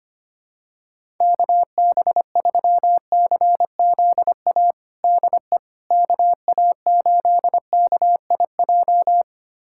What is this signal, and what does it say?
Morse code, heard as a keyed tone, about 25 wpm, 700 Hz. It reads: KB3CZA DE KA8KIJ
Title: USA Callsign DE Callsign - 6 Characters - 25wpm